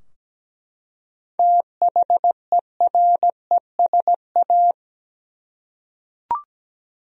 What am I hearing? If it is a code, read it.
Morse code, heard as a keyed tone, about 17 wpm, 700 Hz. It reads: THERESA
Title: USA Common Names - 17wpm